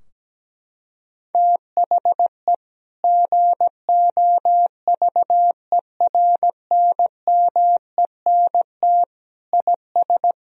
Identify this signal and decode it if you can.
Morse code, heard as a keyed tone, about 17 wpm, 700 Hz. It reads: THE GOVERNMENT IS
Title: Sentences from Top 1000 Words - Review Part 4 - 17wpm